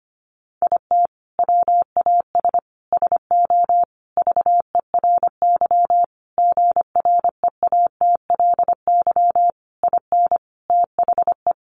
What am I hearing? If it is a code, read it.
Morse code, heard as a keyed tone, about 25 wpm, 700 Hz. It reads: IT WAS SO VERY GREATLY IN THE
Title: Sentences from Top 1400 Words 25wpm